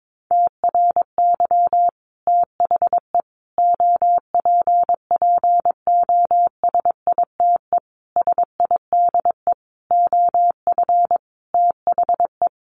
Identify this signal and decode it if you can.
Morse code, heard as a keyed tone, about 22 wpm, 700 Hz. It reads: TRY THE OPPOSITE SIDE OF THE
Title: Sentences from Top 1300 Words 22wpm